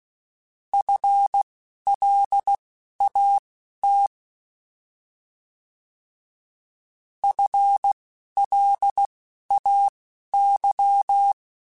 Morse code, heard as a keyed tone, about 16 wpm, spaced as for 11 wpm, 800 Hz. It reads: FLAT FLAY
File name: Alphabetical / Alphabetical mp3s / Fs @10